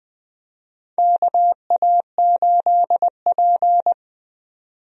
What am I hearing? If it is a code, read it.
Morse code, heard as a keyed tone, about 20 wpm, 700 Hz. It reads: KA8P
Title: USA Call Signs - 4 Characters - Repeated 2x - 20wpm